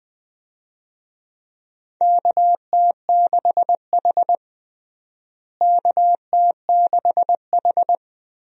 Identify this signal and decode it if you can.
Morse code, heard as a keyed tone, about 21 wpm, 700 Hz. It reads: KT6H KT6H
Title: USA Call Signs - 4 Characters - Repeated 2x - 20wpm